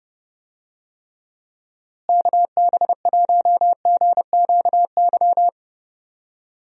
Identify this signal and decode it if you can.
Morse code, heard as a keyed tone, about 30 wpm, 700 Hz. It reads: KB1GQY
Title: USA Call Signs - 6 Characters - Repeated 3x - 30wpm